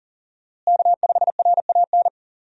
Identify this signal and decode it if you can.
Morse code, heard as a keyed tone, about 40 wpm, 700 Hz. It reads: K5RAN
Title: USA Call Signs - 5 Characters - Repeated 3x - 40wpm